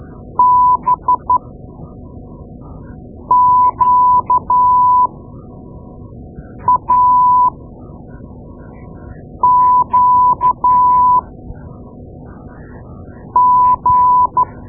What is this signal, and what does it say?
Morse code, heard as a keyed tone, about 12 wpm, spaced as for 4 wpm, 1000 Hz. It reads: BQAQG
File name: Morse Lessons / Morse Lessons / Tape-2A